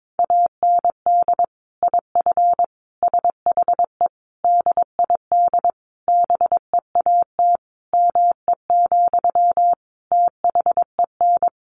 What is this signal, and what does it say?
Morse code, heard as a keyed tone, about 22 wpm, 700 Hz. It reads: AND IF SHE DID BEAT ME, THEN
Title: Sentences from Top 1300 Words 22wpm